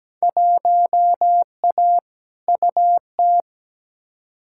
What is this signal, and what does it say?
Morse code, heard as a keyed tone, about 17 wpm, 700 Hz. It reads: 1A UT
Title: Field Day Exchanges - 17wpm